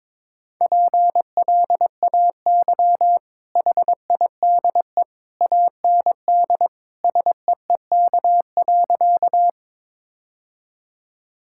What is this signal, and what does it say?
Morse code, heard as a keyed tone, about 21 wpm, 700 Hz. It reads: PLAY HIDE AND SEEK.
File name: practice / 22wpm / Easy Sentences with Hints - 22wpm